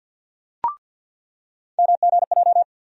Morse code, heard as a keyed tone, about 49 wpm, 700 Hz. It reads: MGJ
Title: USA Call Sign Postfixes - 3 Letters 50wpm